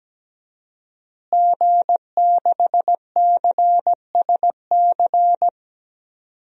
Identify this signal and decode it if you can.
Morse code, heard as a keyed tone, about 17 wpm, 700 Hz. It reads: G6CSC
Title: DXCC-100 Call Signs - 5 Characters - Repeated 3x - 17wpm